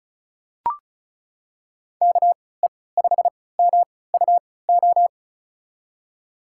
Morse code, heard as a keyed tone, about 36 wpm, spaced as for 20 wpm, 690 Hz. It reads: KE5MUO
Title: USA Call Signs - 3x Character Spacing - 35wpm